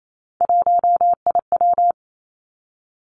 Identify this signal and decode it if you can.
Morse code, heard as a keyed tone, about 28 wpm, 700 Hz. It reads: 1IW
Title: Three Challenging Letter-Number Combinations 28wpm